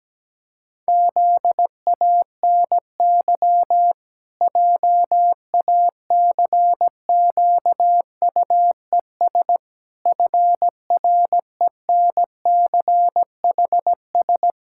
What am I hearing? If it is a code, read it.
Morse code, heard as a keyed tone, about 17 wpm, 700 Hz. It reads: ZANY JACQUES FRENCHS
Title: Pangram Challenge v2 - 17wpm